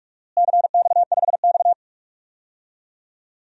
Morse code, heard as a keyed tone, about 41 wpm, 700 Hz. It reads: CX5X
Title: DXCC-100 Call Signs - 4 Characters - Repeated 2x - 45wpm